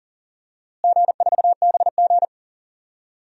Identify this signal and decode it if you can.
Morse code, heard as a keyed tone, about 40 wpm, 700 Hz. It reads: G4BG
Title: DXCC-100 Call Signs - 4 Characters - Repeated 2x - 40wpm